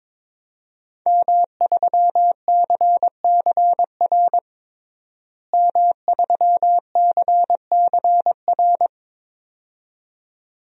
Morse code, heard as a keyed tone, about 22 wpm, 700 Hz. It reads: M3CCR M3CCR
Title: DXCC-100 Call Signs - 5 Characters - Repeated 2x - 22wpm